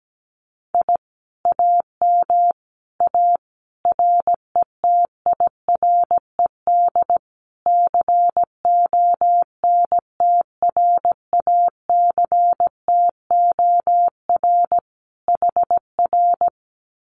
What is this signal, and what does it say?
Morse code, heard as a keyed tone, about 17 wpm, 700 Hz. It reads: I AM A RETIRED CONTRACTOR HR
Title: Common QSO Phrases Encore 17wpm